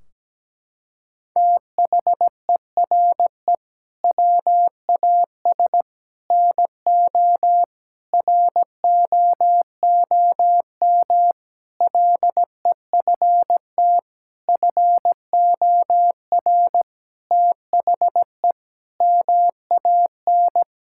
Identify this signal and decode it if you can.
Morse code, heard as a keyed tone, about 17 wpm, 700 Hz. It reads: THERE WAS NO ROOM LEFT FOR THE MAN
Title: Sentences from Top 1600 Words 17wpm